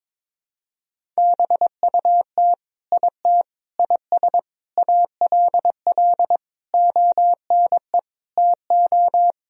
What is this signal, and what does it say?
Morse code, heard as a keyed tone, about 21 wpm, 700 Hz. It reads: BUT IT IS ALL ONE TO
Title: Sentences from Top 1000 Words - Review Part 1 - 22wpm